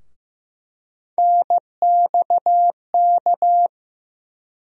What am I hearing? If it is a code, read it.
Morse code, heard as a keyed tone, about 15 wpm, 700 Hz. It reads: NXK
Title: USA Call Sign Postfixes - 3 Letters 15wpm